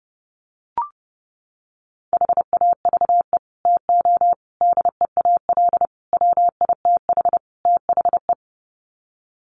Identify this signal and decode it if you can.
Morse code, heard as a keyed tone, about 30 wpm, 700 Hz. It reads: HAVE TO DEAL WITH THE
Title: Sets of 5 Words Encore2 30wpm